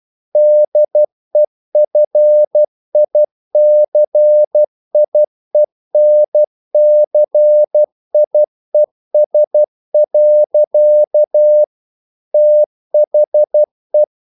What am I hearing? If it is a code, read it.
Morse code, heard as a keyed tone, about 12 wpm, 600 Hz. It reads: DEFICIENCIES. THE
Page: English quotations in CW (telegraphy - morse)